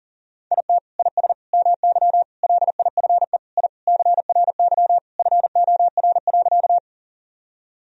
Morse code, heard as a keyed tone, about 40 wpm, 700 Hz. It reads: IT IS MY LIFE I CRY FOR.
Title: Sentences from Top 1000 Words - Review Part 2 - 40wpm